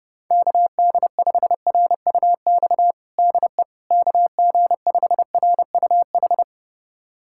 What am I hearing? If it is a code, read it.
Morse code, heard as a keyed tone, about 32 wpm, 700 Hz. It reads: KD5RUX DE KG5RUH
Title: USA Callsign DE Callsign - 6 Characters - 30wpm